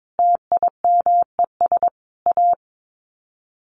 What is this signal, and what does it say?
Morse code, heard as a keyed tone, about 22 wpm, 700 Hz. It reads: TIMES A
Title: Sets of 5 Words Encore2 22wpm